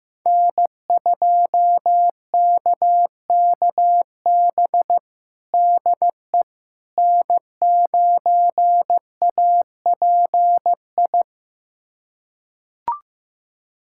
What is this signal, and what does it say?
Morse code, heard as a keyed tone, about 16 wpm, 700 Hz. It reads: N2KKB DE N9API
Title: USA Callsign DE Callsign - 5 Characters - 15wpm